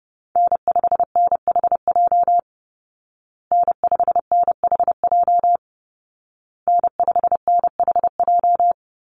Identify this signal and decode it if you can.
Morse code, heard as a keyed tone, about 30 wpm, 700 Hz. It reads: N5NHJ N5NHJ N5NHJ
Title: Call Signs Repeated 2x - 30wpm